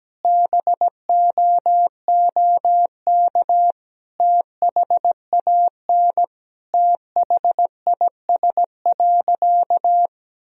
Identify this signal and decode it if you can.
Morse code, heard as a keyed tone, about 17 wpm, 700 Hz. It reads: BOOK THAN THIS.
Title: Sentences from Top 1000 Words - Review Part 3 - 17wpm